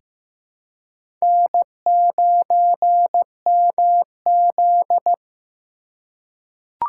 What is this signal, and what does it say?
Morse code, heard as a keyed tone, about 15 wpm, 700 Hz. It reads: N9MZ
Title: USA Call Signs - 4 Characters - 2x Character Spacing - 15wpm